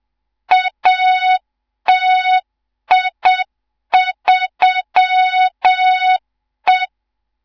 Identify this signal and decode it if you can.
Morse code, heard as a keyed tone, about 7 wpm, 3700 Hz, with noise. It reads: ATI3E
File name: Morze-HA2MN / hang / 18 IRASJELEK2